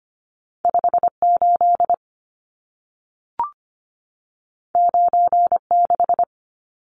Morse code, heard as a keyed tone, about 25 wpm, 700 Hz. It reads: 58 96
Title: Numbers - 2-Digit 25wpm